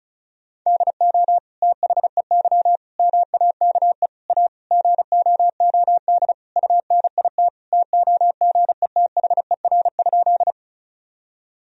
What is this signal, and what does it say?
Morse code, heard as a keyed tone, about 34 wpm, 700 Hz. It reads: DO THEY MAKE A GOOD UNIT TOGETHER?
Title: Sentences from Top 1000 Words - Review Part 2 - 35wpm